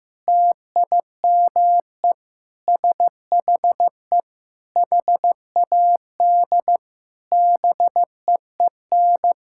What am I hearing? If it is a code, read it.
Morse code, heard as a keyed tone, about 16 wpm, 700 Hz. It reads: TIME SHE HAD BEEN
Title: Sentences from Top 1000 Words - Review Part 6 - 15wpm